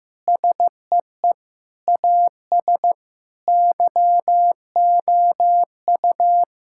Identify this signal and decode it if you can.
Morse code, heard as a keyed tone, about 15 wpm, 700 Hz. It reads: SEE AS YOU
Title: Sentences from Top 1600 Words 15wpm